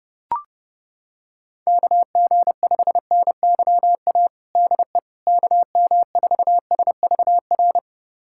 Morse code, heard as a keyed tone, about 30 wpm, 700 Hz. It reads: KG5NYA DE KM4SVR
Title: USA Callsign DE Callsign - 6 Characters - 30wpm